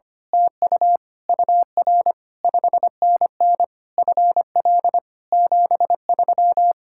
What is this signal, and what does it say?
Morse code, heard as a keyed tone, about 24 wpm, 700 Hz. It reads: TU UR 5NN FL 73
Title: Speed-Racing Common QSO Phrases - 25wpm